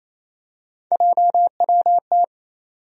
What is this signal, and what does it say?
Morse code, heard as a keyed tone, about 28 wpm, 700 Hz. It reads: JWT
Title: USA Call Sign Postfixes - 3 Letters 28wpm